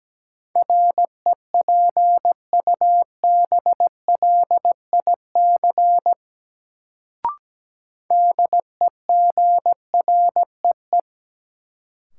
Morse code, heard as a keyed tone, about 17 wpm, 700 Hz. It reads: REPUBLIC DEGREE
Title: Sentences from Top 1500 Words - New Word Drill - 17wpm